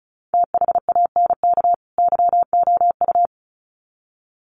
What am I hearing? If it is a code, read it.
Morse code, heard as a keyed tone, about 34 wpm, 700 Hz. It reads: THANK YOU
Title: Sets of 2 Words 35wpm